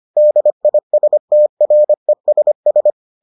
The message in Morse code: DISTRESS

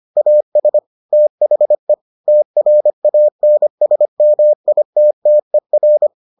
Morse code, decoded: AS THE TRANSMITTER

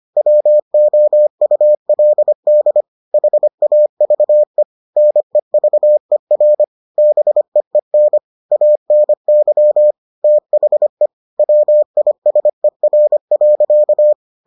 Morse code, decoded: WOULD HAVE NEVER BEEN ANY THE WISER.

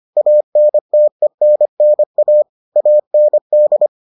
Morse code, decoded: ANTENNA AND

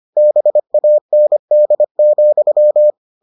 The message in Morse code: BAND,

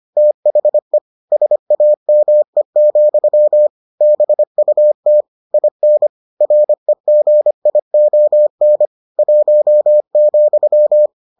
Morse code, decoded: THE SAME, BUT IN REGION 1,

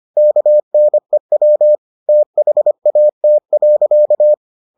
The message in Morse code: KNEW THAT.

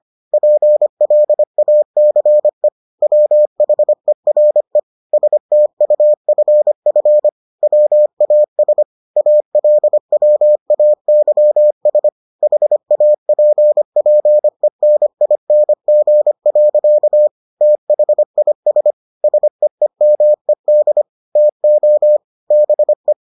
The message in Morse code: PLACE WHERE STUFF WAS ALWAYS HAPPENING. THIS SEEMED TO BE

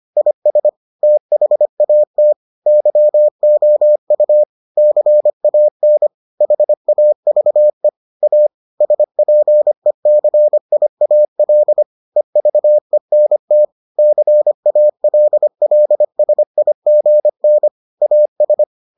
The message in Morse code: IS THAT YOU CAN HAVE A SPECIAL EVENT CALLSIGN AS

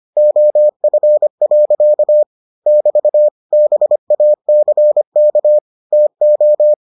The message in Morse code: OF. <BT> BACK TO

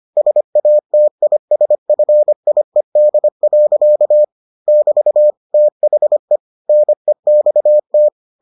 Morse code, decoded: SATISFIED. <BT> THE NEXT

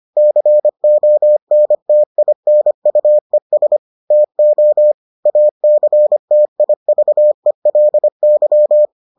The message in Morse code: CONTINUES TO ACTIVELY

